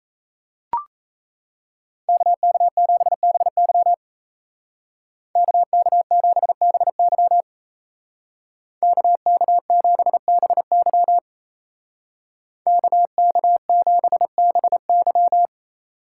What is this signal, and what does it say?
Morse code, heard as a keyed tone, about 36 wpm, 700 Hz. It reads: KK7BY KK7BY KK7BY KK7BY